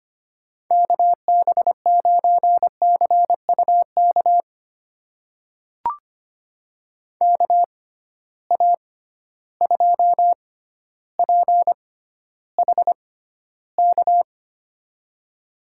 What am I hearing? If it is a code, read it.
Morse code, heard as a keyed tone, about 25 wpm, spaced as for 13 wpm, 700 Hz. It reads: KB9CUK KA2PHK
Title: USA Call Signs - 6 Characters - 6x Character Spacing - 25wpm